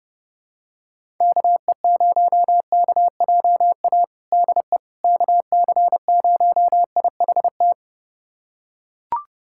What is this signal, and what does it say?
Morse code, heard as a keyed tone, about 30 wpm, 700 Hz. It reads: KE0KJA DE KC0IHT
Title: USA Callsign DE Callsign - 6 Characters - 30wpm